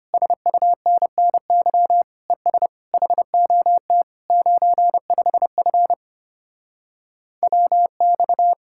Morse code, heard as a keyed tone, about 33 wpm, 700 Hz. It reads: SUNNY ES HOT 95F WX